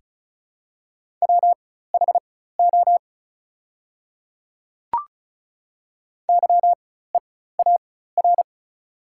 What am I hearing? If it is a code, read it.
Morse code, heard as a keyed tone, about 34 wpm, spaced as for 16 wpm, 700 Hz. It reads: WHO YEAR
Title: 4x Character Spacing - Top 200 Words - 35wpm